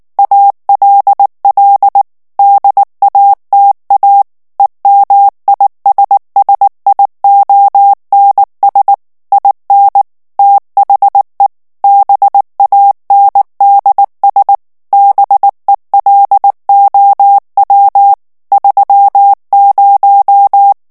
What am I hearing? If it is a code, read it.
Morse code, heard as a keyed tone, about 19 wpm, 800 Hz. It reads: ALL DATA EMISSIONS IN THE BANDS BELOW 30
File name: ANGLAIS texte clair 3 vitesse 18 mots minute